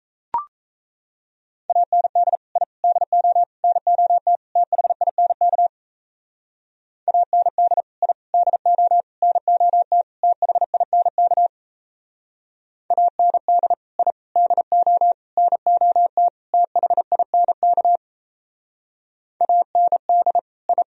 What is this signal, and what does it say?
Morse code, heard as a keyed tone, about 34 wpm, 700 Hz. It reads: AND I DO NOT THINK AND I DO NOT THINK AND I DO NOT THINK AND I
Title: Speed-Racing - Sets of 5 Words 28wpm